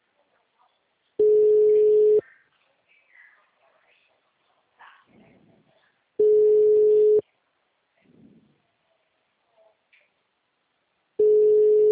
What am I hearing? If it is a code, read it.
Ringback tone (Europe, 425 Hz).
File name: out- -2010- - - .246